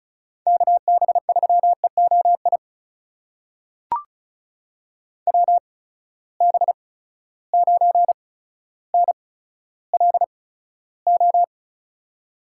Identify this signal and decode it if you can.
Morse code, heard as a keyed tone, about 36 wpm, spaced as for 15 wpm, 700 Hz. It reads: KB3EOI WB9NLO
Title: USA Call Signs - 6 Characters - 8x Character Spacing - 35wpm